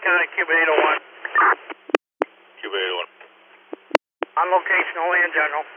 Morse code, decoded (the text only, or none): none